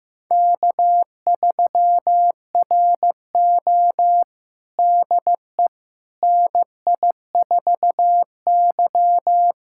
Morse code, heard as a keyed tone, about 15 wpm, 700 Hz. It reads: K3RO DE NI4Y
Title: USA Callsign DE Callsign - 4 Characters - 15wpm